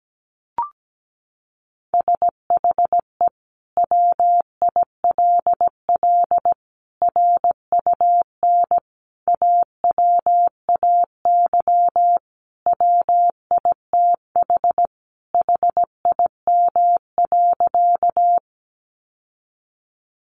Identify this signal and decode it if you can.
Morse code, heard as a keyed tone, about 17 wpm, 700 Hz. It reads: SHE WILL RUN AWAY WITH HIM.
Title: Sentences from Top 600 Words 17wpm